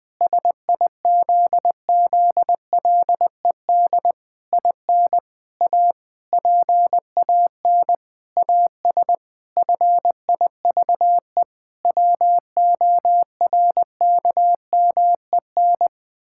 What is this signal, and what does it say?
Morse code, heard as a keyed tone, about 20 wpm, 700 Hz. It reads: SIZZLED IN A PAN AS FIVE WORKMEN
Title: Pangram Challenge v2 - 20wpm